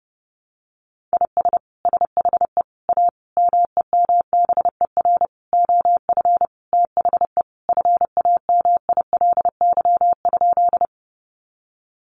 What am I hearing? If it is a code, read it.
Morse code, heard as a keyed tone, about 30 wpm, 700 Hz. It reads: IS SHE A MEMBER OF THE FAMILY?
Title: Sentences from Top 1100 Words 30wpm